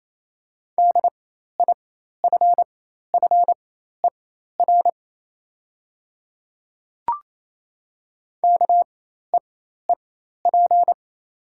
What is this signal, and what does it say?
Morse code, heard as a keyed tone, about 28 wpm, spaced as for 13 wpm, 700 Hz. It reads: DIFFER KEEP
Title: 4x Character Spacing - Top 500 Words - 28wpm